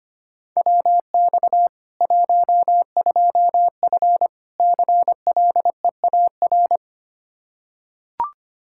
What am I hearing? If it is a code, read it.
Morse code, heard as a keyed tone, about 25 wpm, 700 Hz. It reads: WX 12F CLEAR
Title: Practice QSO - Exchange 2 - Info - 25wpm